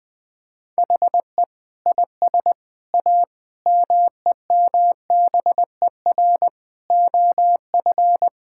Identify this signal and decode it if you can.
Morse code, heard as a keyed tone, about 20 wpm, 700 Hz. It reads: HE IS A MEMBER OF
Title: Sentences with 3-Word Phrases - 20wpm